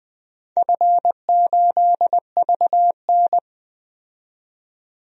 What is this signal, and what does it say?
Morse code, heard as a keyed tone, about 20 wpm, 700 Hz. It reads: F8VN
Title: DXCC-100 Call Signs - 4 Characters - 20wpm